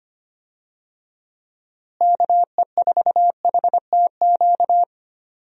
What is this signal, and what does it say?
Morse code, heard as a keyed tone, about 25 wpm, 700 Hz. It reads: KE4HTQ
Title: USA Call Signs - 6 Characters - Repeated 2x - 25wpm